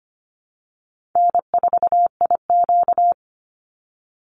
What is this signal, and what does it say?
Morse code, heard as a keyed tone, about 25 wpm, 700 Hz. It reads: N4IQ
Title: Easy Call Signs Repeated 2x - 25wpm